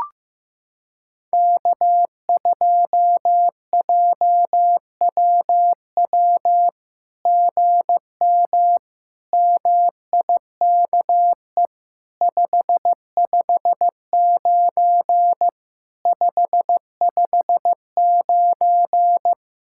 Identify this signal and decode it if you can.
Morse code, heard as a keyed tone, about 15 wpm, 700 Hz. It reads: K2JWW GM MIKE 559 559
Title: SOTA QSO Practice 15wpm